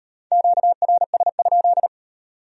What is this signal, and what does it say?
Morse code, heard as a keyed tone, about 36 wpm, 700 Hz. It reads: QRS?